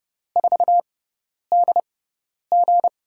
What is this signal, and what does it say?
Morse code, heard as a keyed tone, about 30 wpm, spaced as for 10 wpm, 700 Hz. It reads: 4DG
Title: USA Call Signs - 6x Character Spacing - 30wpm